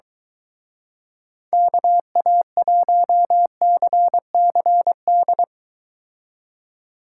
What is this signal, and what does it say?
Morse code, heard as a keyed tone, about 23 wpm, 700 Hz. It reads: KA1CCD